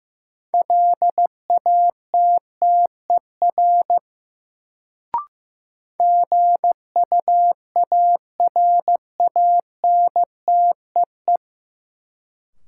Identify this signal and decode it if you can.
Morse code, heard as a keyed tone, about 15 wpm, 700 Hz. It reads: LATTER GUARANTEE
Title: Sentences from Top 1600 Words - New Word Drill - 15wpm